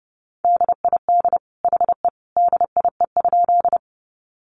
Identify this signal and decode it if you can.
Morse code, heard as a keyed tone, about 30 wpm, 700 Hz. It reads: DID HE DIE?
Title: Sentences from Top 1300 Words 30wpm